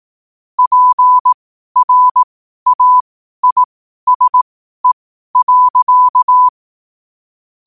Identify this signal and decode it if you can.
Morse code, heard as a keyed tone, about 18 wpm, spaced as for 12 wpm, 1000 Hz. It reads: PRAISE.
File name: qotdfiles / 12wpm